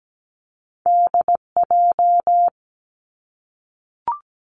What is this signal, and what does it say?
Morse code, heard as a keyed tone, about 17 wpm, 700 Hz. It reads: DJ
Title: Two Challenging Letter-Number Combinations 17wpm